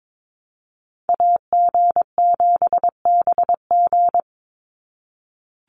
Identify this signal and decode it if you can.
Morse code, heard as a keyed tone, about 22 wpm, 700 Hz. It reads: AG7BG
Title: USA Call Signs 22wpm